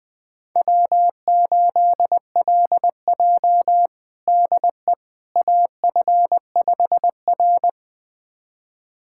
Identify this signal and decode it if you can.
Morse code, heard as a keyed tone, about 20 wpm, 700 Hz. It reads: W8LJ DE AF5R
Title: USA Callsign DE Callsign - 4 Characters - 20wpm